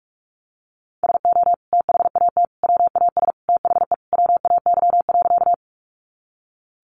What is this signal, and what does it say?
Morse code, heard as a keyed tone, about 44 wpm, 700 Hz. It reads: SO THAT WAS THE WAY.
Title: Sentences from Top 100 Words 45wpm